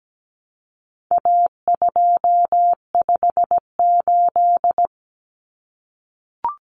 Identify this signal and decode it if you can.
Morse code, heard as a keyed tone, about 17 wpm, 700 Hz. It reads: A258
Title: Common Cut Numbers - 4-Digit 17wpm